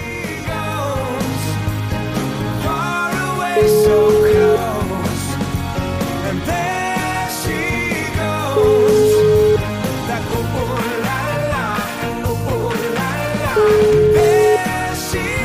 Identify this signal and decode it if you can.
Ringback tone (Europe, 425 Hz), with noise.